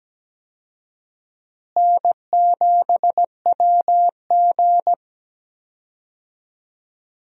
Morse code, heard as a keyed tone, about 17 wpm, 700 Hz. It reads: N7WG
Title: USA Call Signs - 4 Characters - Repeated 2x - 17wpm